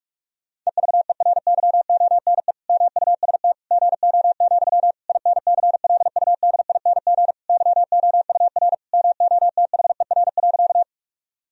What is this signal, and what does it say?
Morse code, heard as a keyed tone, about 47 wpm, 700 Hz. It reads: EVERYONE MUST GO, INCLUDING YOUR MOTHER.
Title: Sentences from Top 1000 Words - Review Part 4 - 45wpm